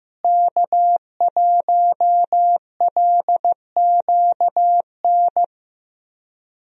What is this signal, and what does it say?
Morse code, heard as a keyed tone, about 15 wpm, 700 Hz. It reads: K1LQN
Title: USA Call Signs - 5 Characters - Repeated 2x - 15wpm